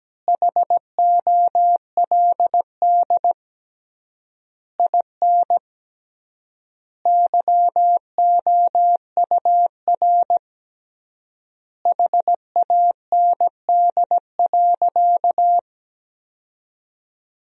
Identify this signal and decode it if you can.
Morse code, heard as a keyed tone, about 17 wpm, 700 Hz. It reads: HOLD IN YOUR HAND.
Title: 3x Word-Spacing - Sentences from Top 400 Words - 17wpm